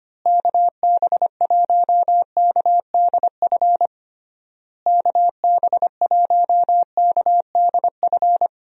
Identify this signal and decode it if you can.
Morse code, heard as a keyed tone, about 25 wpm, 700 Hz. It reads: KB1KDF KB1KDF
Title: USA Call Signs - 6 Characters - Repeated 3x - 25wpm